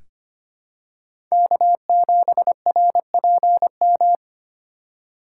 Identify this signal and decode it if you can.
Morse code, heard as a keyed tone, about 25 wpm, 700 Hz. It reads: K7RPM
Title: USA Call Signs - 4x Character Spacing - 25wpm